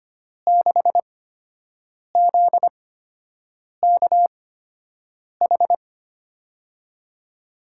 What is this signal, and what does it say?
Morse code, heard as a keyed tone, about 25 wpm, spaced as for 7 wpm, 700 Hz. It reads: 6ZKH
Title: USA Call Signs - 5 Characters - 8x Character Spacing - 25wpm